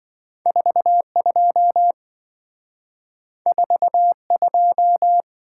Morse code, heard as a keyed tone, about 27 wpm, 700 Hz. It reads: 42 42